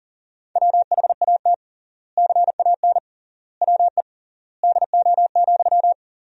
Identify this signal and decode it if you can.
Morse code, heard as a keyed tone, about 40 wpm, 700 Hz. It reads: WHAT CAN WE DO,
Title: 3x Word-Spacing - Sentences from Top 100 Words - 40wpm